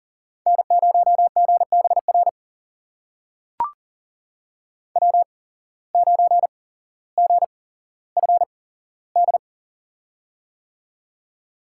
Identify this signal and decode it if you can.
Morse code, heard as a keyed tone, about 40 wpm, spaced as for 17 wpm, 700 Hz. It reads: N0GBR W9GFD
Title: USA Call Signs - 8x Character Spacing - 40wpm